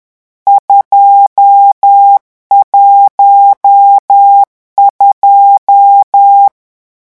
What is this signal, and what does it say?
Morse code, heard as a keyed tone, about 10 wpm, 800 Hz. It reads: 212
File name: morse mp3 / lecon10-vitesse 10